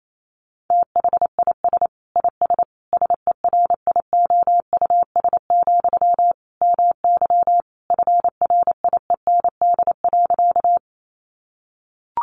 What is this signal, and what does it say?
Morse code, heard as a keyed tone, about 28 wpm, 700 Hz. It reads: THIS IS SERIOUS, MY FRIEND.
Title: Sentences from Top 1100 Words 28wpm